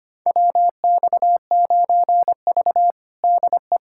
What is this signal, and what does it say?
Morse code, heard as a keyed tone, about 25 wpm, 700 Hz. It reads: WX9V DE
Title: USA Callsign DE Callsign - 4 Characters - 25wpm